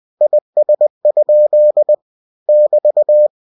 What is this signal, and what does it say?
Morse code, heard as a keyed tone, about 20 wpm, 600 Hz. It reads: IS? <BT>
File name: cw-books / winnie-the-pooh / winnie-20wpm-page-0050